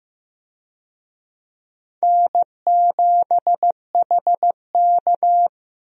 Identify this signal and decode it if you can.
Morse code, heard as a keyed tone, about 15 wpm, 700 Hz. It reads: N7HK
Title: USA Call Signs - 4 Characters - Repeated 2x - 15wpm